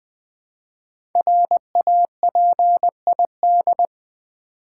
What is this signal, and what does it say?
Morse code, heard as a keyed tone, about 20 wpm, 700 Hz. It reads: RAPID
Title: Build A Word - 1 Letter At A Time - 5-Letter Words - 20wpm